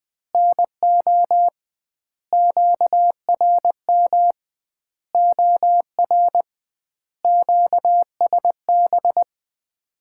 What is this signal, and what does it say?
Morse code, heard as a keyed tone, about 20 wpm, 700 Hz. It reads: NO QRM OR QSB
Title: Common QSO Phrases - 2x Word-Spacing - 20wpm